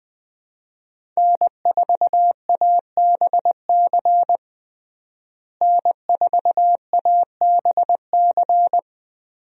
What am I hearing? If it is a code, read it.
Morse code, heard as a keyed tone, about 21 wpm, 700 Hz. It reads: N4ABC N4ABC
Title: USA Call Signs - 5 Characters - Repeated 3x - 20wpm